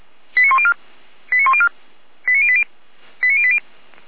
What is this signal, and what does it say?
Five-tone selective call (ZVEI), heard as quick runs of five tones, tones of about 70 ms. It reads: 89104 89104 89080 89080
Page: SDR/ZVEI